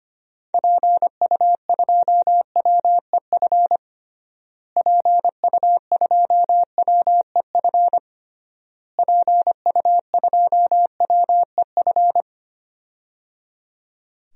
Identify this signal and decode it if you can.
Morse code, heard as a keyed tone, about 25 wpm, 700 Hz. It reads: PU2WEF PU2WEF PU2WEF
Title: DXCC-100 Call Signs - 6 Characters - Repeated 3x - 25wpm